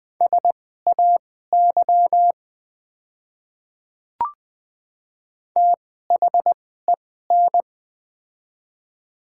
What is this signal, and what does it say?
Morse code, heard as a keyed tone, about 20 wpm, spaced as for 14 wpm, 700 Hz. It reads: SAY THEN
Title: 2x Character Spacing - Top 200 Words - 20wpm